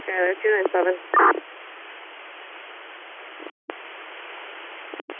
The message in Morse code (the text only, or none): none